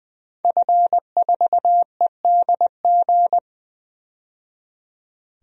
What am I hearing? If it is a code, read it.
Morse code, heard as a keyed tone, about 20 wpm, 700 Hz. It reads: F4EDG
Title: DXCC-100 Call Signs - 5 Characters - 20wpm